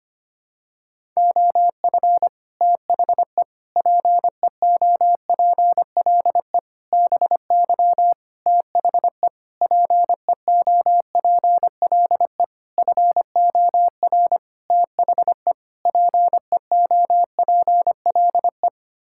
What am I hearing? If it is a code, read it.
Morse code, heard as a keyed tone, about 25 wpm, 700 Hz. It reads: OF THE PEOPLE BY THE PEOPLE FOR THE PEOPLE
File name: Sets of 9 Words - 25wpm